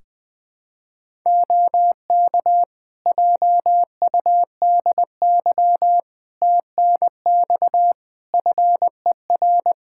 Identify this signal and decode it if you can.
Morse code, heard as a keyed tone, about 20 wpm, 700 Hz. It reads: OK JUDY TNX FER
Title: Practice QSO - Exchange 4 - Ending - 20wpm